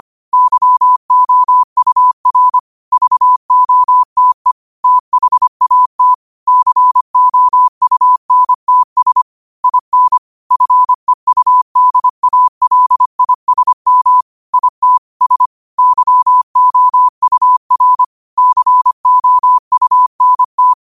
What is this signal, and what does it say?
Morse code, heard as a keyed tone, about 25 wpm, 1000 Hz. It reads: YOUR VOTE THAT COUNTS IN FEUDALISM IT S YOUR COUNT
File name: qotdfiles / 25wpm